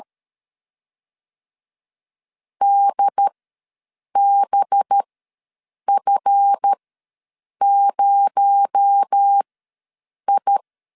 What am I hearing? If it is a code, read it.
Morse code, heard as a keyed tone, about 13 wpm, spaced as for 7 wpm, 800 Hz. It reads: DBF0I
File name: 7 GF9Z ws 12wpm